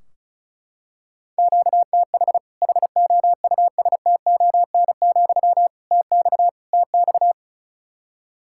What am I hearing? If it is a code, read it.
Morse code, heard as a keyed tone, about 36 wpm, 700 Hz. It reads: QTH HOUSTON, TX TX
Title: Practice QSO - Exchange 1 - Beginning - 35wpm